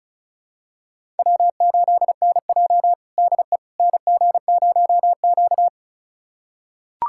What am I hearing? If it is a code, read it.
Morse code, heard as a keyed tone, about 36 wpm, 700 Hz. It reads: W8NJ DE NG0Q
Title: USA Callsign DE Callsign - 4 Characters - 35wpm